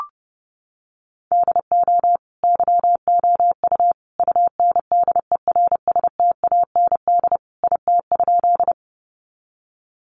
Morse code, heard as a keyed tone, about 30 wpm, 700 Hz. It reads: DO YOU UNDERSTAND IT?
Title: Sentences from Top 600 Words 30wpm